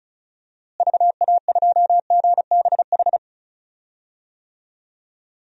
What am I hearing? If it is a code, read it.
Morse code, heard as a keyed tone, about 34 wpm, 700 Hz. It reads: VA2GBH
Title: DXCC-100 Call Signs - 6 Characters - Repeated 3x - 35wpm